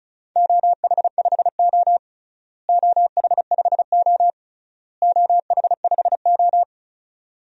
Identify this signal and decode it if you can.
Morse code, heard as a keyed tone, about 34 wpm, 700 Hz. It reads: OH5O OH5O OH5O
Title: DXCC-100 Call Signs - 4 Characters - Repeated 3x - 35wpm